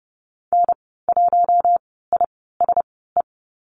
Morse code, heard as a keyed tone, about 30 wpm, spaced as for 17 wpm, 700 Hz. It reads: N1ISE
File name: Lesson 15g - Number 5 - Call Signs 30wpm